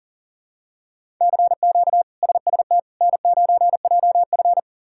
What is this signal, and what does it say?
Morse code, heard as a keyed tone, about 40 wpm, 700 Hz. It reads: CQ SST N9JF
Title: K1USN SST QSO Practice - 40wpm